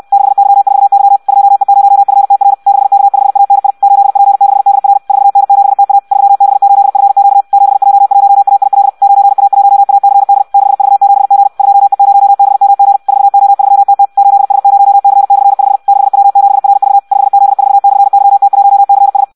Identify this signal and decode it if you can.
Morse code, heard as a keyed tone, about 64 wpm, 800 Hz. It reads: 6Y4Y 2E0FTR 6K5ANI 8P6DR 4N7TA 3B8FQ 7J1AIL 9A1ACD 6V7G 2E0BMG 4O3TT 8S0C/5 6W7RV 5Z4/YT1CS
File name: Audio practica CW / qrq-tx-66 2